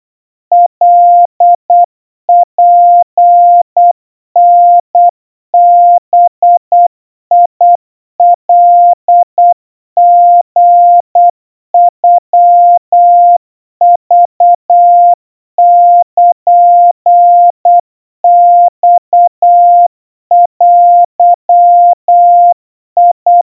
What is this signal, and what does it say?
Morse code (Japanese Wabun code), heard as a keyed tone, about 8 wpm, 700 Hz. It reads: カツタバカリノクルマデ
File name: 30 cpm 700 Hz Crispy